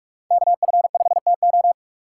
Morse code, heard as a keyed tone, about 46 wpm, 700 Hz. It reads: KF5TO
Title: USA Call Signs - 5 Characters - 3x Character Spacing - 45wpm